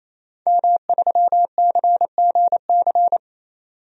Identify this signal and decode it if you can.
Morse code, heard as a keyed tone, about 27 wpm, 700 Hz. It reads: M3CGC
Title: DXCC-100 Call Signs - 5 Characters - 28wpm